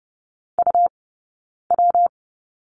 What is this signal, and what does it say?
Morse code, heard as a keyed tone, about 30 wpm, spaced as for 9 wpm, 690 Hz. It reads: UW
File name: Lesson 19d - Letter w - Two Characters 30wpm